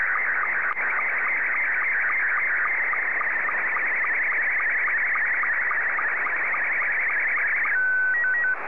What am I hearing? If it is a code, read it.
SSTV (slow-scan television), Scottie 2.